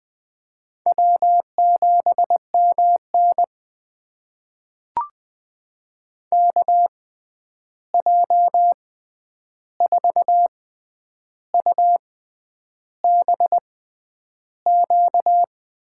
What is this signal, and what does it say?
Morse code, heard as a keyed tone, about 20 wpm, spaced as for 9 wpm, 700 Hz. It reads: W7MN KJ4UBQ
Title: USA Call Signs - 6x Character Spacing - 20wpm